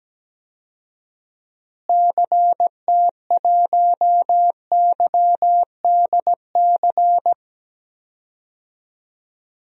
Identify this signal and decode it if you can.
Morse code, heard as a keyed tone, about 17 wpm, 700 Hz. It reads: CT1YDC
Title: DXCC-100 Call Signs - 6 Characters - Repeated 3x - 17wpm